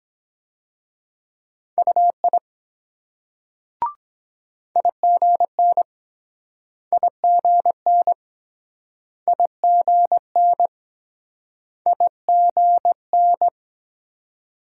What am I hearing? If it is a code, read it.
Morse code, heard as a keyed tone, about 23 wpm, 700 Hz. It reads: UI IGN IGN IGN IGN